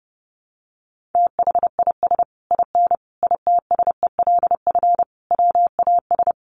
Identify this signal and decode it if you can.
Morse code, heard as a keyed tone, about 30 wpm, 700 Hz. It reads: THIS IN ITSELF WAS
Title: Sentences from Top 1400 Words 30wpm